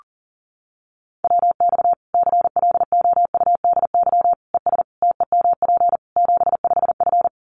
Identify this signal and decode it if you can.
Morse code, heard as a keyed tone, about 40 wpm, 700 Hz. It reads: WX CLOUDY ES TEMP 75F
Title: Common QSO Phrases 40wpm